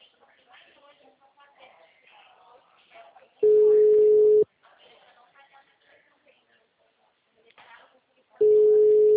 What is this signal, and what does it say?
Ringback tone (Europe, 425 Hz), with speech.
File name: out- -2010- - - .23456